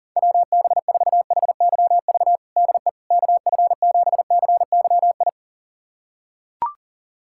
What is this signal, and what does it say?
Morse code, heard as a keyed tone, about 40 wpm, 700 Hz. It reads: WB4HYV DE KF7CYI
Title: USA Callsign DE Callsign - 6 Characters - 40wpm